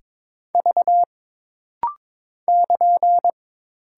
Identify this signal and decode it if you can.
Morse code, heard as a keyed tone, about 22 wpm, 700 Hz. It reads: V <KN>
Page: This rapid-fire timing is geared to push your ICR (Instant Character Recognition) skills.